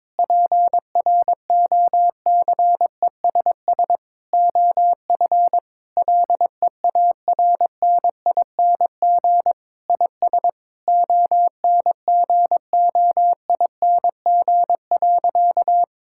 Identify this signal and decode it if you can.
Morse code, heard as a keyed tone, about 22 wpm, 700 Hz. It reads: PROCESS OF LEARNING IS ONGOING.
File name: practice / 22wpm / Sentences with 3-Word Phrases - 22wpm